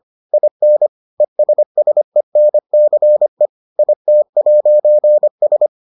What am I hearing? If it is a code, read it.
Morse code, heard as a keyed tone, about 25 wpm, 600 Hz. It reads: IN ESSENCE IT'S